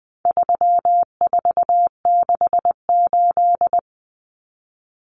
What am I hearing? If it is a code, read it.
Morse code, heard as a keyed tone, about 20 wpm, 700 Hz. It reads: 3468
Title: Numbers - 4-Digit 20wpm